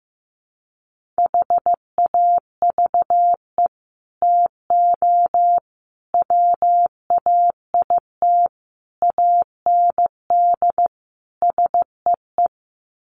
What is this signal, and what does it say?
Morse code, heard as a keyed tone, about 14 wpm, 700 Hz. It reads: HAVE TO WAIT AND SEE
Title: Sets of 5 Words Encore2 15wpm